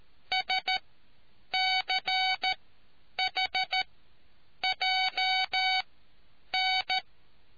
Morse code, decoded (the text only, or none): SCHJN